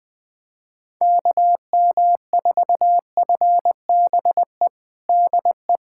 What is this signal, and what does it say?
Morse code, heard as a keyed tone, about 21 wpm, 700 Hz. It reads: KM4FBE DE
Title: USA Callsign DE Callsign - 6 Characters - 20wpm